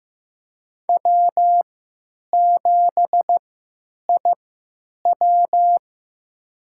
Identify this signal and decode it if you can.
Morse code, heard as a keyed tone, about 15 wpm, spaced as for 9 wpm, 700 Hz. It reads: W7IW
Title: USA Call Signs - 3x Character Spacing - 15wpm